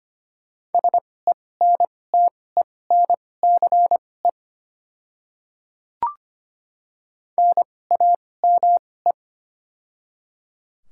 Morse code, heard as a keyed tone, about 25 wpm, spaced as for 18 wpm, 700 Hz. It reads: SENTENCE NAME
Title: 2x Character Spacing - Top 200 Words - 25wpm